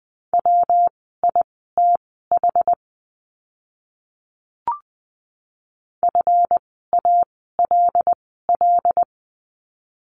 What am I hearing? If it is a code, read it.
Morse code, heard as a keyed tone, about 20 wpm, spaced as for 14 wpm, 700 Hz. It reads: WITH FALL
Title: 139 Lesson 23f - Letter f - Words 20wpm